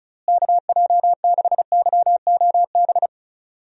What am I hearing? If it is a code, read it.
Morse code, heard as a keyed tone, about 34 wpm, 700 Hz. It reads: KJ6YOB
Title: USA Call Signs - 6 Characters - Repeated 2x - 35wpm